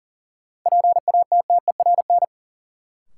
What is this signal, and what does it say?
Morse code, heard as a keyed tone, about 40 wpm, 700 Hz. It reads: PATTERN
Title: Sentences from Top 1000 Words - New Word Drill - 40wpm